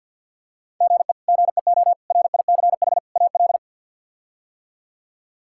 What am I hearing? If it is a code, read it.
Morse code, heard as a keyed tone, about 47 wpm, 700 Hz. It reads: GE GEO RICH AL
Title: K1USN SST QSO Practice - 50wpm